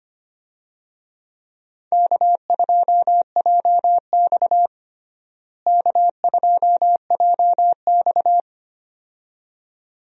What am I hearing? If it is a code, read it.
Morse code, heard as a keyed tone, about 25 wpm, 700 Hz. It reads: K2JX K2JX
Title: USA Call Signs - 4 Characters - Repeated 2x - 25wpm